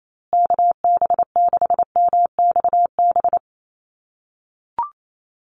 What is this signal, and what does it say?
Morse code, heard as a keyed tone, about 28 wpm, 700 Hz. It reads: KB6MXB
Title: USA Call Signs Repeated 2x 28wpm